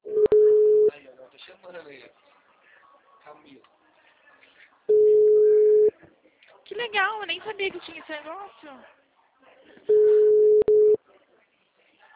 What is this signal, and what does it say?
Ringback tone (Europe, 425 Hz), with speech.